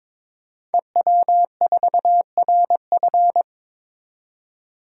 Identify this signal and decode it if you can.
Morse code, heard as a keyed tone, about 22 wpm, 700 Hz. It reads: EW4RF
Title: DXCC-100 Call Signs - 5 Characters - 22wpm